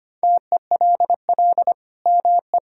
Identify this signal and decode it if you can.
Morse code, heard as a keyed tone, about 25 wpm, 700 Hz. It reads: TELL ME
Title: Sentences from Top 1000 Words - Review Part 5 - 25wpm